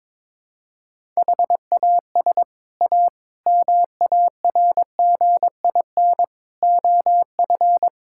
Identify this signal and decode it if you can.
Morse code, heard as a keyed tone, about 22 wpm, 700 Hz. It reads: HAS A MARGIN OF
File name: practice / 22wpm / Sets of 6 Words - 22wpm